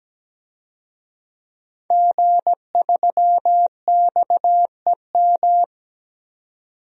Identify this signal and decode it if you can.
Morse code, heard as a keyed tone, about 17 wpm, 700 Hz. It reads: G3XEM
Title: DXCC-100 Call Signs - 5 Characters - 17wpm